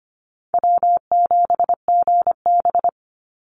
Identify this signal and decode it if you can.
Morse code, heard as a keyed tone, about 25 wpm, 700 Hz. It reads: W7GB
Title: Call Signs - 25wpm